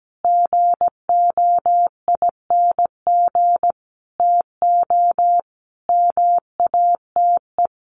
Morse code, heard as a keyed tone, about 17 wpm, 700 Hz. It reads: GOING TO MATE
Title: A Sound Of Thunder 17wpm